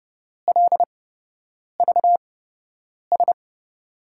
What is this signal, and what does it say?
Morse code, heard as a keyed tone, about 30 wpm, spaced as for 8 wpm, 700 Hz. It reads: LVS
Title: USA Call Signs - 8x Character Spacing - 30wpm